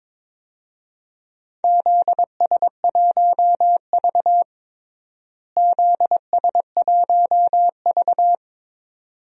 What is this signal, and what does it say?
Morse code, heard as a keyed tone, about 22 wpm, 700 Hz. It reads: ZS1V ZS1V
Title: DXCC-100 Call Signs - 4 Characters - Repeated 2x - 22wpm